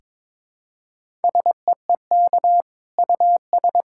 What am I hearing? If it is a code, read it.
Morse code, heard as a keyed tone, about 21 wpm, 700 Hz. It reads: SEEK US
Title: Sentences from Top 1600 Words 22wpm